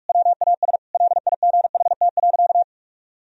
Morse code, heard as a keyed tone, about 41 wpm, 700 Hz. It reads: WAS LIGHT.
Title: Sentences from Top 1000 Words - Review Part 1 - 45wpm